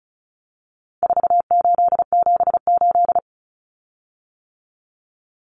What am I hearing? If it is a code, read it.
Morse code, heard as a keyed tone, about 34 wpm, 700 Hz. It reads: 4878
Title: Common Cut Numbers - 4-Digit 35wpm